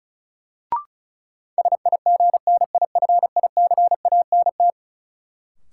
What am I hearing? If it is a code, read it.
Morse code, heard as a keyed tone, about 36 wpm, 700 Hz. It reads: SIGNIFICANT
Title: Sentences from Top 1600 Words - New Word Drill Alt - 35wpm